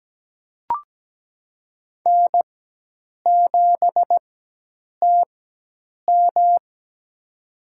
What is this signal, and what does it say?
Morse code, heard as a keyed tone, about 17 wpm, spaced as for 8 wpm, 700 Hz. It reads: N7TM
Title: USA Call Signs - 4 Characters - 4x Character Spacing - 17wpm